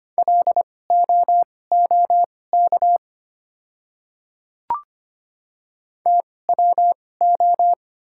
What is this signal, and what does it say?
Morse code, heard as a keyed tone, about 25 wpm, spaced as for 18 wpm, 690 Hz. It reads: LOOK TWO
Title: 2x Character Spacing - Top 100 Words - 25wpm